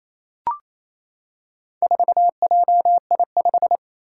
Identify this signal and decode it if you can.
Morse code, heard as a keyed tone, about 28 wpm, 700 Hz. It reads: 4JI5
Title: Four Challenging Letter-Number Combinations 28wpm